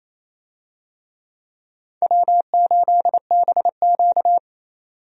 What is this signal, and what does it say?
Morse code, heard as a keyed tone, about 28 wpm, 700 Hz. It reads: W8BQ
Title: USA Call Signs - 4 Characters - 28wpm